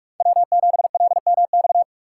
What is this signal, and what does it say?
Morse code, heard as a keyed tone, about 58 wpm, 700 Hz. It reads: W7LMX